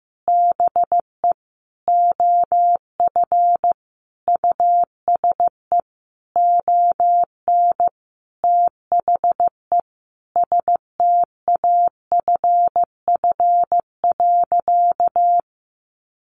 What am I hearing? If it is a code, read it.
Morse code, heard as a keyed tone, about 15 wpm, 700 Hz. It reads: BE OF USE ON THE STAFF.
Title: Sentences from Top 1200 Words 15wpm